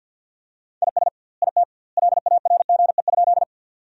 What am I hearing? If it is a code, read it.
Morse code, heard as a keyed tone, about 50 wpm, 700 Hz. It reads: IS IT LARGE?
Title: 2x Word-Spacing - Sentences from Top 200 Words - 50wpm